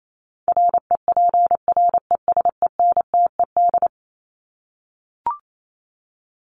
Morse code, heard as a keyed tone, about 28 wpm, 700 Hz. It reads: REPRESENTED
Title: ED Suffix - Top 100 Words with Base Words 28wpm